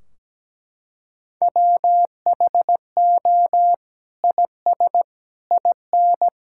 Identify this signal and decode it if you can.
Morse code, heard as a keyed tone, about 17 wpm, 700 Hz. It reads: WHO IS IN
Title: Sentences from Top 1600 Words 17wpm